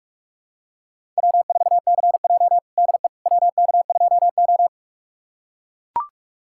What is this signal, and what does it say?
Morse code, heard as a keyed tone, about 42 wpm, 700 Hz. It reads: W4CJ DE WK2O
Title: USA Callsign DE Callsign - 4 Characters - 45wpm